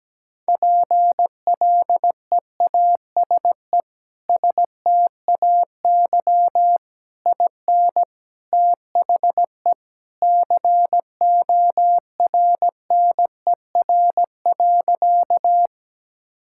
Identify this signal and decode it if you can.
Morse code, heard as a keyed tone, about 17 wpm, 700 Hz. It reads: PLEASE STAY IN THE CORNER.
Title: Sentences from Top 1600 Words 17wpm